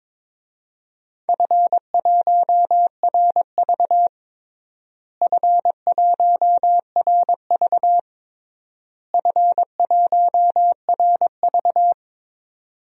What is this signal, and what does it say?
Morse code, heard as a keyed tone, about 22 wpm, 700 Hz. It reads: F1RV F1RV F1RV
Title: DXCC-100 Call Signs - 4 Characters - Repeated 3x - 22wpm